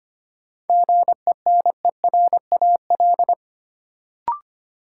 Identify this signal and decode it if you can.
Morse code, heard as a keyed tone, about 25 wpm, 700 Hz. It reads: GENERAL
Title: Sentences from Top 600 Words - New Word Drill - 25wpm